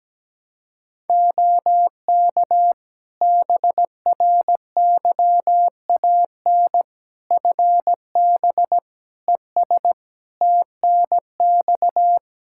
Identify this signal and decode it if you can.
Morse code, heard as a keyed tone, about 17 wpm, 700 Hz. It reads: OK BRYAN FB ES TNX
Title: Practice QSO - Exchange 2 - Info - 17wpm